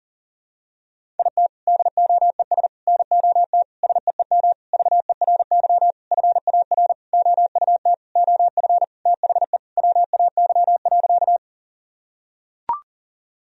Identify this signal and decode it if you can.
Morse code, heard as a keyed tone, about 37 wpm, 700 Hz. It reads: IT DOES NOT SEEM VERY FAR OUT OF THE WAY.
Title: Sentences from Top 1000 Words - Review Part 1 - 40wpm